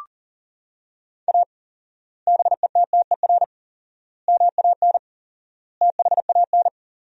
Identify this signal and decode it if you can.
Morse code, heard as a keyed tone, about 37 wpm, 700 Hz. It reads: A BETTER MAN THAN
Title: 4x Word-Spacing - Sentences from Top 400 Words - 40wpm